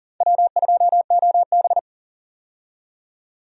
Morse code, heard as a keyed tone, about 40 wpm, 700 Hz. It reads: W2OB
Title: USA Call Signs - 4 Characters - 40wpm